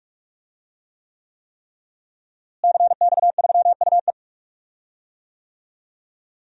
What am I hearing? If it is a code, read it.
Morse code, heard as a keyed tone, about 44 wpm, 700 Hz. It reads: CX3UE
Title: DXCC-100 Call Signs - 5 Characters - Repeated 3x - 45wpm